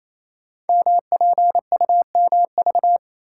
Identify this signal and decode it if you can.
Morse code, heard as a keyed tone, about 28 wpm, 690 Hz. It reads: MPUMV
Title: Code Groups - 5 Letters - 28wpm